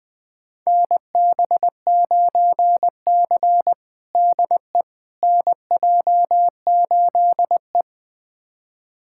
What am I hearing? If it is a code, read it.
Morse code, heard as a keyed tone, about 20 wpm, 700 Hz. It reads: NB9C DE NJ8E
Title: USA Callsign DE Callsign - 4 Characters - 20wpm